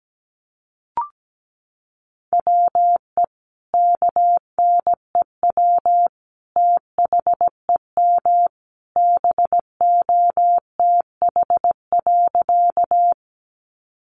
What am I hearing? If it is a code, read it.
Morse code, heard as a keyed tone, about 17 wpm, 700 Hz. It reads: WE KNEW THEM BOTH.
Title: Sentences from Top 400 Words 17wpm